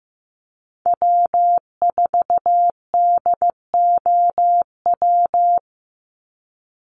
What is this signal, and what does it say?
Morse code, heard as a keyed tone, about 15 wpm, 700 Hz. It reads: W4DOW
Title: NA SOTA Chaser Callsigns 15wpm